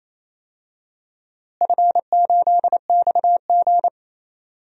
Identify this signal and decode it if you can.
Morse code, heard as a keyed tone, about 27 wpm, 700 Hz. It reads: F8XG
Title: DXCC-100 Call Signs - 4 Characters - Repeated 2x - 28wpm